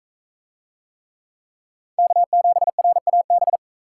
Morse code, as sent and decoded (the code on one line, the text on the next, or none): -.- --... .-. .- -...
K7RAB